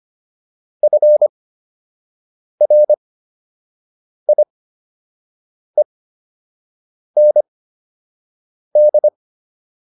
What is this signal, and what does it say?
Morse code, heard as a keyed tone, about 25 wpm, spaced as for 6 wpm, 600 Hz. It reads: FRIEND